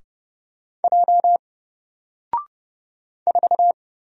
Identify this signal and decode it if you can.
Morse code, heard as a keyed tone, about 30 wpm, 700 Hz. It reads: J 4
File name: Single-Letter-Number-30wpm